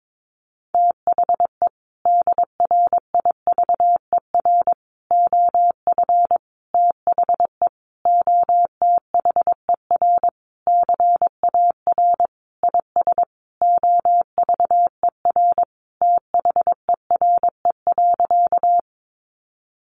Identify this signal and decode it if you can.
Morse code, heard as a keyed tone, about 22 wpm, 700 Hz. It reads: THE DRIVER OF THE OTHER CAR IS OVER THERE.
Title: Sentences from Top 1000 Words 22wpm